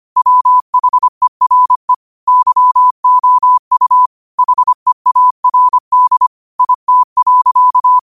Morse code, heard as a keyed tone, about 25 wpm, 1000 Hz. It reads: WHERE YOU HEARD IT.
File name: qotdfiles / 25wpm